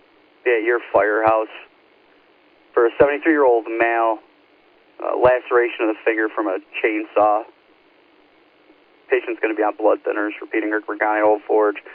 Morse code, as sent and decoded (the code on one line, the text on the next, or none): none